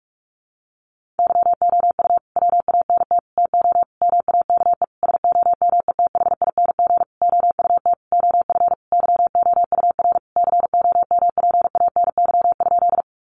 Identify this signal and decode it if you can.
Morse code, heard as a keyed tone, about 44 wpm, 700 Hz. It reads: YOU WANT TO MAKE SOMETHING OUT OF YOUR COMPANY?
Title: Sentences from Top 600 Words 45wpm